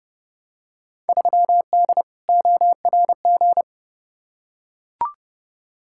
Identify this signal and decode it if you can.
Morse code, heard as a keyed tone, about 30 wpm, 700 Hz. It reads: 3D ORG
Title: Field Day Exchanges Encore - 30wpm